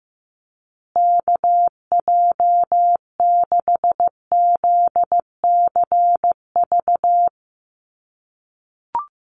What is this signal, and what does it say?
Morse code, heard as a keyed tone, about 15 wpm, 700 Hz. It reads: KJ6ZCV
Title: USA Call Signs Repeated 3x 15wpm